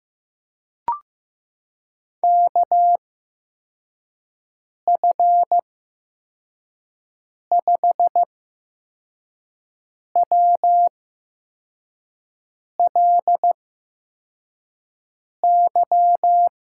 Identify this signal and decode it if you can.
Morse code, heard as a keyed tone, about 16 wpm, spaced as for 4 wpm, 700 Hz. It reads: KF5WLY
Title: USA Call Signs - 6 Characters - 8x Character Spacing - 15wpm